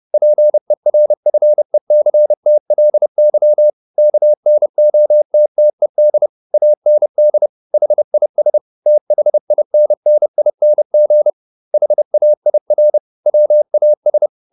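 Morse code, heard as a keyed tone, about 30 wpm, 600 Hz. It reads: PERFECTLY KNOTTED AND HIS THINNING HAIR WAS
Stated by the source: Morse code at 30 wpm, the novel Never by Ken Follett in English, from CW med Gnister.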